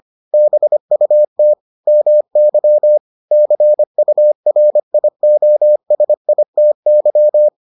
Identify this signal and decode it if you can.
Morse code, heard as a keyed tone, about 25 wpm, 600 Hz. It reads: BUT MY CURIOSITY